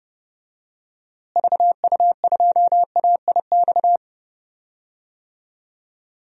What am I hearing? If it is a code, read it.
Morse code, heard as a keyed tone, about 30 wpm, 700 Hz. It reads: VU2AIX
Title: DXCC-100 Call Signs - 6 Characters - Repeated 2x - 30wpm